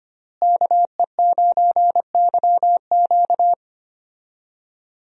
Morse code, heard as a keyed tone, about 25 wpm, 700 Hz. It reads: KE9YQ
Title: USA Call Signs - 5 Characters - Repeated 2x - 25wpm